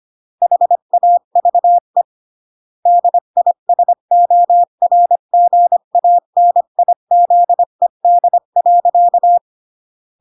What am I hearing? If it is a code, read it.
Morse code, heard as a keyed tone, about 25 wpm, 700 Hz. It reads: HAVE DISORGANIZED.